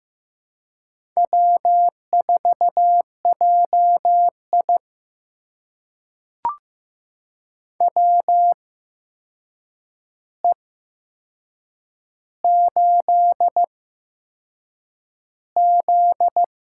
Morse code, heard as a keyed tone, about 15 wpm, spaced as for 6 wpm, 700 Hz. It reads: W4JI WE8Z
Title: USA Call Signs - 4 Characters - 8x Character Spacing - 15wpm